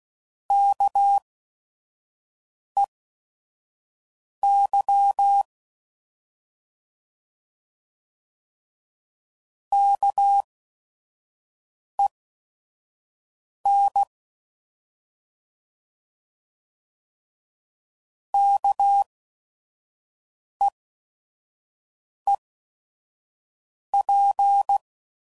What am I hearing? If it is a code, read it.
Morse code, heard as a keyed tone, about 16 wpm, spaced as for 5 wpm, 800 Hz. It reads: KEY KEN KEEP